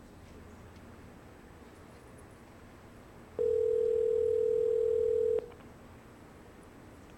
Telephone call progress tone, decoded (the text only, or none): ringback tone